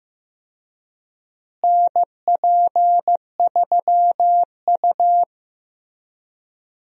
Morse code, heard as a keyed tone, about 15 wpm, 700 Hz. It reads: NP3U
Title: USA Call Signs - 4 Characters - Repeated 2x - 15wpm